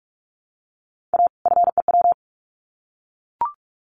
Morse code, heard as a keyed tone, about 44 wpm, 700 Hz. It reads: A FEW
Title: Sets of 2 Words 45wpm